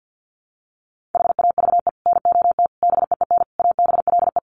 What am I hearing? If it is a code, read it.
Morse code, heard as a keyed tone, about 48 wpm, 700 Hz. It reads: HAVE NOT BEEN ABLE
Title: Sets of 5 Words Encore 50wpm